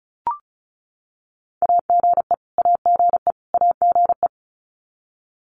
Morse code, heard as a keyed tone, about 34 wpm, 700 Hz. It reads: AGE AGE AGE
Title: Top 500 Words Repeated 35wpm